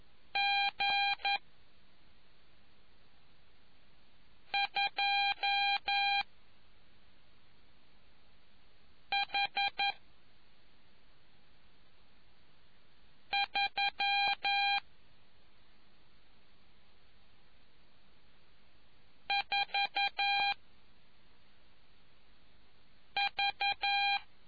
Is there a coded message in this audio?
Yes: Morse code from 0.4 to 24.2 s, heard as a keyed tone, about 11 wpm, 800 Hz. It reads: G 2 H 3 4 V